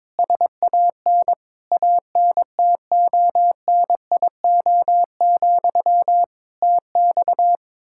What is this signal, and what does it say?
Morse code, heard as a keyed tone, about 21 wpm, 700 Hz. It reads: SAN ANTONIO, TX
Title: K1USN SST QSO Practice 22wpm